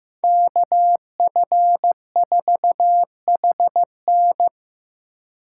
Morse code, heard as a keyed tone, about 18 wpm, 700 Hz. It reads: KF4HN
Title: Speed Racing - USA Call Signs 15wpm